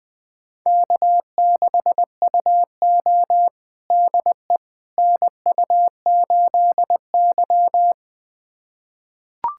Morse code, heard as a keyed tone, about 21 wpm, 700 Hz. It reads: K6UO DE NU8Y
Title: USA Callsign DE Callsign - 4 Characters - 20wpm